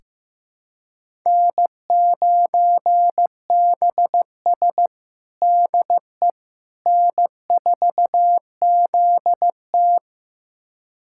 Morse code, heard as a keyed tone, about 15 wpm, 700 Hz. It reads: N9BS DE N4ZT
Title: USA Callsign DE Callsign - 4 Characters - 15wpm